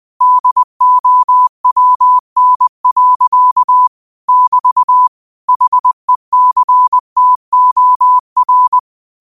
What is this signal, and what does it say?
Morse code, heard as a keyed tone, about 20 wpm, 1000 Hz. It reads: DOWN. <BT> HECTOR